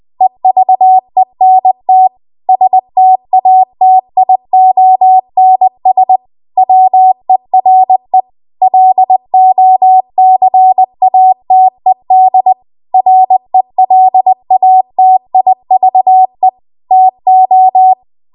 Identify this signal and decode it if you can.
Morse code, heard as a keyed tone, about 20 wpm, 750 Hz. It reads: EVENT STATIONS WERE LOCATED RELATIVE TO
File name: Archive / 20 WPM / 20WPM